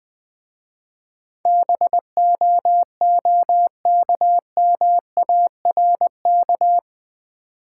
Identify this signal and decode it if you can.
Morse code, heard as a keyed tone, about 20 wpm, 700 Hz. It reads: BOOKMARK
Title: Easy Words with Hints - 20wpm